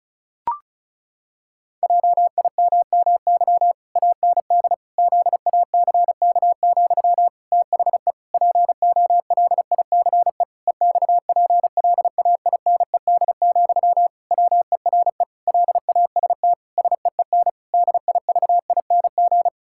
Morse code, heard as a keyed tone, about 34 wpm, 700 Hz. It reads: JIMMY AND ZACK, THE POLICE EXPLAINED, WERE LAST SEEN DIVING
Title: Pangram Challenge v2 - 35wpm